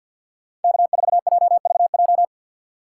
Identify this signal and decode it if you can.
Morse code, heard as a keyed tone, about 53 wpm, 690 Hz. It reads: K4JVJ
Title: USA Call Signs - 6x Character Spacing - 50wpm